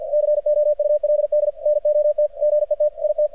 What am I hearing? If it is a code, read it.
Morse code, heard as a keyed tone, about 50 wpm, 600 Hz, with radio noise. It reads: COULD NOT GET IT
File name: archived / CRSnet